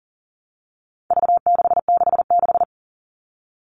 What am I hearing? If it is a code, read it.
Morse code, heard as a keyed tone, about 40 wpm, 700 Hz. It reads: V666
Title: All Cut Numbers - 4-Digit 40wpm